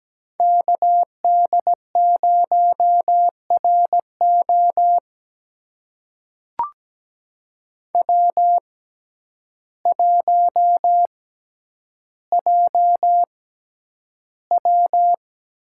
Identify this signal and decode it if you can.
Morse code, heard as a keyed tone, about 17 wpm, spaced as for 9 wpm, 700 Hz. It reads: KD0RO W1JW
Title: USA Call Signs - 5 Characters - 6x Character Spacing - 17wpm